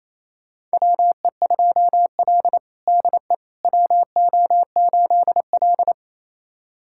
Morse code, heard as a keyed tone, about 28 wpm, 700 Hz. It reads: WE2L DE WO8L
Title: USA Callsign DE Callsign - 4 Characters - 28wpm